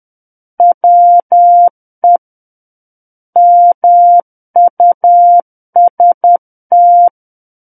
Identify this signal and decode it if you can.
Morse code, heard as a keyed tone, about 10 wpm, 700 Hz. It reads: WE MUST
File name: Book One / 01-Chapter 1 - 10 WPM